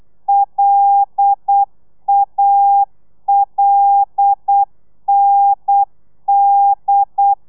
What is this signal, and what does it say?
Morse code, heard as a keyed tone, about 8 wpm, 800 Hz. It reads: LALND
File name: telegrafia / i2tfj / Lez5 35